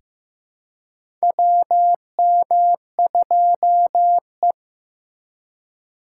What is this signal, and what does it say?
Morse code, heard as a keyed tone, about 15 wpm, 700 Hz. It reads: WM2E
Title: USA Call Signs - 3x Character Spacing - 15wpm